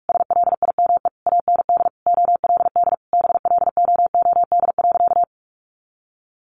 Morse code, heard as a keyed tone, about 44 wpm, 700 Hz. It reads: SLIME AND OLD BLOOD.
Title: A Sound Of Thunder 45wpm